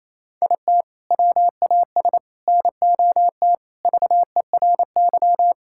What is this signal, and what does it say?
Morse code, heard as a keyed tone, about 28 wpm, 700 Hz. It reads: IT WAS NOT VERY
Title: Sentences from Top 1000 Words - Review Part 5 - 28wpm